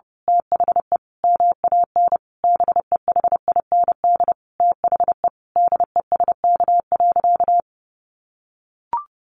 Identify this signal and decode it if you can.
Morse code, heard as a keyed tone, about 30 wpm, 700 Hz. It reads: THE MAN BEHIND THE DESK.
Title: A Sound Of Thunder 30wpm